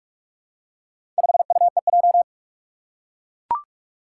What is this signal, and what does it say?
Morse code, heard as a keyed tone, about 41 wpm, 700 Hz. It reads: 5UEJ
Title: Four Challenging Letter-Number Combinations 45wpm